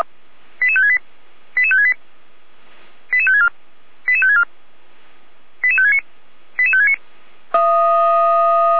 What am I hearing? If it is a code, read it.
Five-tone selective call (ZVEI), heard as quick runs of five tones, tones of about 80 ms. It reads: 88568 88568 88564 88564 88569 88569
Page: /455/SDR/ZVEI/